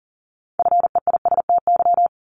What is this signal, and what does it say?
Morse code, heard as a keyed tone, about 40 wpm, 700 Hz. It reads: FEISTY
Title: Multi-letter Phonemes 40wpm